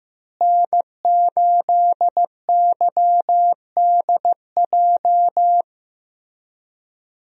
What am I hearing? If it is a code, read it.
Morse code, heard as a keyed tone, about 15 wpm, 700 Hz. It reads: N8YDJ
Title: USA Call Signs - 5 Characters - Repeated 3x - 15wpm